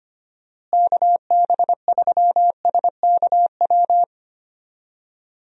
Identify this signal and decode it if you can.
Morse code, heard as a keyed tone, about 25 wpm, 700 Hz. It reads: KB3SKW
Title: USA Call Signs - 6 Characters - Repeated 3x - 25wpm